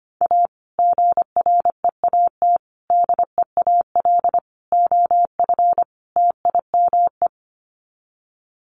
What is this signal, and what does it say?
Morse code, heard as a keyed tone, about 25 wpm, 700 Hz. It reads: A GREAT DEAL OF TIME
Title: Sets of 5 Words Encore2 25wpm